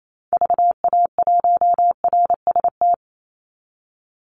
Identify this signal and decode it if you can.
Morse code, heard as a keyed tone, about 28 wpm, 700 Hz. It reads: VA1RST
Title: Call Signs Repeated 3x - 28wpm